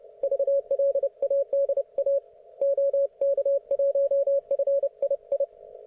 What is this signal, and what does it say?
Morse code, heard as a keyed tone, about 29 wpm, 560 Hz. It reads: VLADA OK1FII
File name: Archiv / CUC133 / oK1fhi cul133